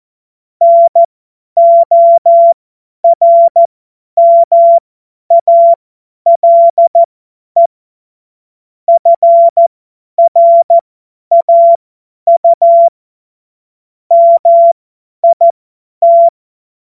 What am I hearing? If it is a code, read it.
Morse code, heard as a keyed tone, about 14 wpm, spaced as for 10 wpm, 680 Hz. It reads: NORMALE FRAU MIT
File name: Morsekurs / Witze / Witz 27. April 2021, 50 F70